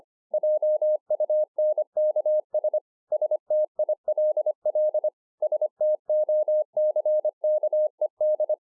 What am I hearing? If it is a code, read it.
Morse code, heard as a keyed tone, about 25 wpm, 610 Hz. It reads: JUNKS STILL STOCKED